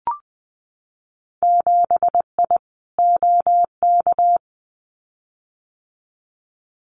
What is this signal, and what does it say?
Morse code, heard as a keyed tone, about 20 wpm, 700 Hz. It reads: 7I OK
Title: Winter Field Day Exchanges Encore 20wpm